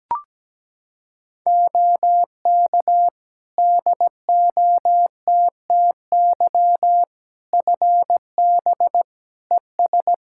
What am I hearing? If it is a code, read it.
Morse code, heard as a keyed tone, about 17 wpm, 700 Hz. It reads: OK DOTTY FB ES
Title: K1USN SST QSO Practice 17wpm